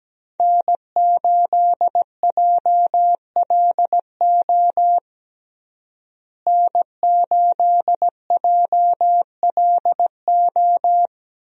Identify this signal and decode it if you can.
Morse code, heard as a keyed tone, about 17 wpm, 700 Hz. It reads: N8JLO N8JLO
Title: USA Call Signs - 5 Characters - Repeated 3x - 17wpm